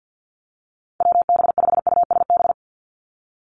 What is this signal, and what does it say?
Morse code, heard as a keyed tone, about 50 wpm, 700 Hz. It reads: WB5USB
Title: NA SOTA Activator Callsigns 50wpm